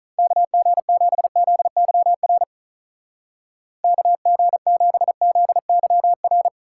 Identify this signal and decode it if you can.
Morse code, heard as a keyed tone, about 46 wpm, 700 Hz. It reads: KG7ZYR KG7ZYR